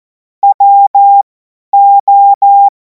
Morse code, heard as a keyed tone, about 14 wpm, spaced as for 10 wpm, 800 Hz. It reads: WO
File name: Morsekurs / Witze / Witz 24. Februar 2026 50 F70